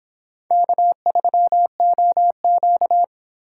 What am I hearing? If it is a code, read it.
Morse code, heard as a keyed tone, about 26 wpm, 700 Hz. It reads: K3OQ